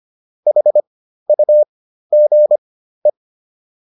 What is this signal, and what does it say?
Morse code, heard as a keyed tone, about 25 wpm, spaced as for 13 wpm, 600 Hz. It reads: HUGE